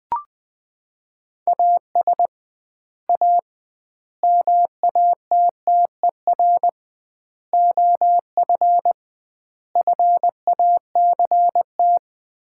Morse code, heard as a keyed tone, about 21 wpm, 700 Hz. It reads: AS A MATTER OF FACT
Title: 2x Word-Spacing - Sets of 5 Words - 20wpm